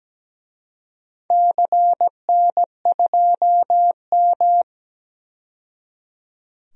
Morse code, heard as a keyed tone, about 17 wpm, 700 Hz. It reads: CN2M
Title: DXCC-100 Call Signs - 4 Characters - 17wpm